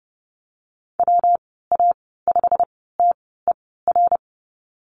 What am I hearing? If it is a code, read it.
Morse code, heard as a keyed tone, about 30 wpm, spaced as for 17 wpm, 700 Hz. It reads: WA5TER
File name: Lesson 19g - Letter w - Call Signs 30wpm